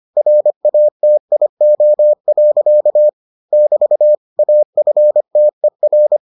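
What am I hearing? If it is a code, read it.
Morse code, heard as a keyed tone, about 25 wpm, 600 Hz. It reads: RATIO. <BT> AFTER
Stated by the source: Morse edition of the Foundations of Amateur Radio episodes.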